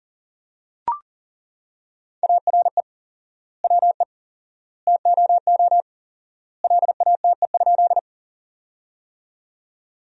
Morse code, heard as a keyed tone, about 40 wpm, 700 Hz. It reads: ARE WE TOO LATE?
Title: 4x Word-Spacing - Sentences from Top 300 Words - 40wpm